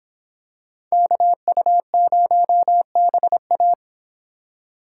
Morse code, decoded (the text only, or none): KU0BA